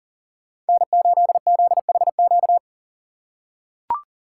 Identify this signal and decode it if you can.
Morse code, heard as a keyed tone, about 40 wpm, 700 Hz. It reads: N8ZHQ
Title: USA Call Signs - 5 Characters - 6x Character Spacing - 40wpm